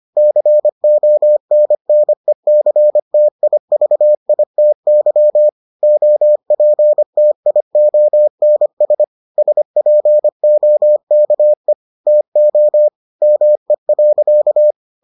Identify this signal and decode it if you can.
Morse code, heard as a keyed tone, about 25 wpm, 600 Hz. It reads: CONNECTIVITY OPTIONS SPOKE TO ME.